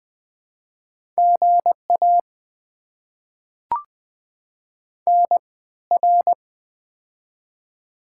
Morse code, heard as a keyed tone, about 20 wpm, spaced as for 14 wpm, 690 Hz. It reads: GA NR
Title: Top 90 QSO Elements - 3x Character Spacing - 20wpm